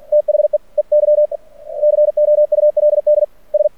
Morse code, heard as a keyed tone, about 49 wpm, 600 Hz. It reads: THEEY COULD S
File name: archived / CRSnet